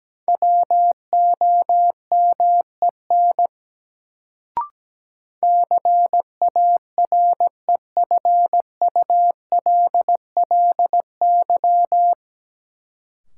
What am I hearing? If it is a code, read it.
Morse code, heard as a keyed tone, about 17 wpm, 700 Hz. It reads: WOMEN CAREFULLY
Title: Sentences from Top 1300 Words - New Word Drill - 17wpm